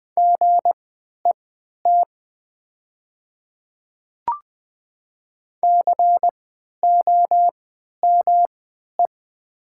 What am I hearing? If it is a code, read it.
Morse code, heard as a keyed tone, about 20 wpm, spaced as for 11 wpm, 700 Hz. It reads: GET COME
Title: 3x Character Spacing - Top 200 Words - 20wpm